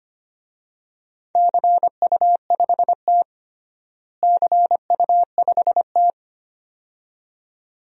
Morse code, heard as a keyed tone, about 25 wpm, 700 Hz. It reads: CU5T CU5T
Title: DXCC-100 Call Signs - 4 Characters - Repeated 2x - 25wpm